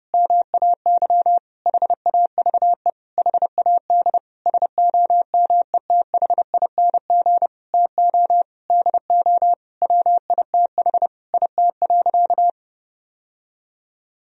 Morse code, heard as a keyed tone, about 32 wpm, 700 Hz. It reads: MAY HAVE HAD SOMETHING TO DO WITH IT.
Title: Sentences from Top 1000 Words - Review Part 2 - 30wpm